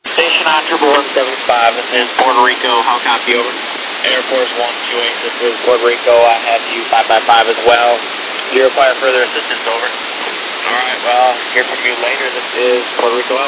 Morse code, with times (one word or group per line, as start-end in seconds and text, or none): none